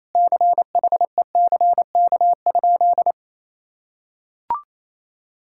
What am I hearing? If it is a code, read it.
Morse code, heard as a keyed tone, about 28 wpm, 700 Hz. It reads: CHECK?
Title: Sentences from Top 1000 Words - Review Part 6 - 28wpm